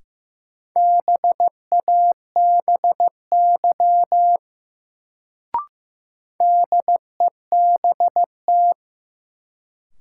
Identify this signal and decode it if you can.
Morse code, heard as a keyed tone, about 15 wpm, 700 Hz. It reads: BABY DEBT
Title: Sentences from Top 1300 Words - New Word Drill Alt - 15wpm